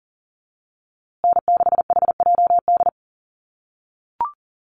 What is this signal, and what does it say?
Morse code, heard as a keyed tone, about 40 wpm, 700 Hz. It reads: N6HJD
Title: USA Call Signs Repeated 3x 40wpm